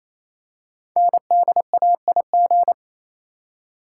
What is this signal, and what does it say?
Morse code, heard as a keyed tone, about 29 wpm, 700 Hz. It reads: NDAIG
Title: Code Groups - 5 Letters - 28wpm